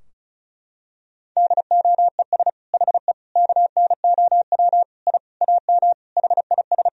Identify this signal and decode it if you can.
Morse code, heard as a keyed tone, about 34 wpm, 700 Hz. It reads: DOES HE KNOW I AM HIS
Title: Sentences from Top 1000 Words - Review Part 2 - 35wpm